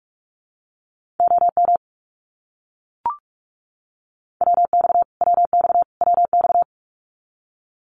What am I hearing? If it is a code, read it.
Morse code, heard as a keyed tone, about 44 wpm, 700 Hz. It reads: OM WX WX WX
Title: Abbreviations Repeated 3x - 45wpm